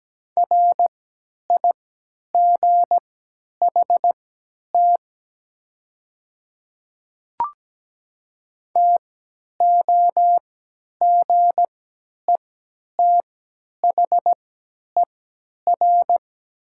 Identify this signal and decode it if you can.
Morse code, heard as a keyed tone, about 17 wpm, spaced as for 10 wpm, 700 Hz. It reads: RIGHT TOGETHER
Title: 3x Character Spacing - Top 300 Words - 17wpm